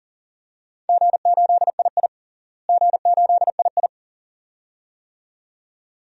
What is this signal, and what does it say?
Morse code, heard as a keyed tone, about 37 wpm, 700 Hz. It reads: G8II G8II
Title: DXCC-100 Call Signs - 4 Characters - Repeated 2x - 40wpm